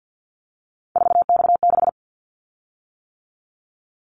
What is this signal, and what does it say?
Morse code, heard as a keyed tone, about 50 wpm, 700 Hz. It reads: 4X6
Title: Callsign Prefixes for Top 50 DXCC - 50wpm